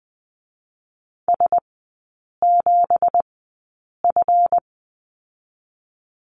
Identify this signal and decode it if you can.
Morse code, heard as a keyed tone, about 20 wpm, spaced as for 8 wpm, 700 Hz. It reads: S7F
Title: 218 Lesson 35e - Number 0 - Three Characters 20wpm